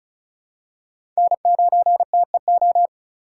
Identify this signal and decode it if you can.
Morse code, heard as a keyed tone, about 36 wpm, 700 Hz. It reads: N9TEO
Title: USA Call Signs - 5 Characters - 6x Character Spacing - 35wpm